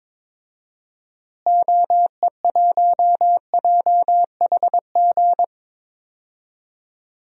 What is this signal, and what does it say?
Morse code, heard as a keyed tone, about 22 wpm, 700 Hz. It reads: OE1JHG
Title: DXCC-100 Call Signs - 6 Characters - Repeated 3x - 22wpm